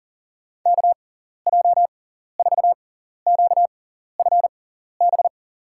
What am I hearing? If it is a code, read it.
Morse code, heard as a keyed tone, about 40 wpm, spaced as for 14 wpm, 700 Hz. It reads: KJ4QFB
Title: USA Call Signs - 6 Characters - 6x Character Spacing - 40wpm